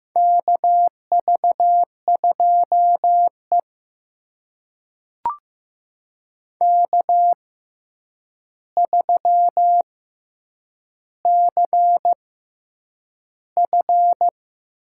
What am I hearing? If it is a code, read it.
Morse code, heard as a keyed tone, about 15 wpm, spaced as for 8 wpm, 700 Hz. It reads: KV2E K3CF
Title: USA Call Signs - 4 Characters - 6x Character Spacing - 15wpm